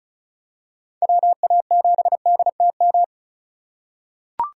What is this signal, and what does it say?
Morse code, heard as a keyed tone, about 36 wpm, 700 Hz. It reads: WA7DTM
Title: USA Call Signs - 6 Characters - 4x Character Spacing - 35wpm